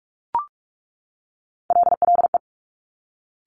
Morse code, heard as a keyed tone, about 46 wpm, 690 Hz. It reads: LLE
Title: Top Three Letter Combinations in Words 45wpm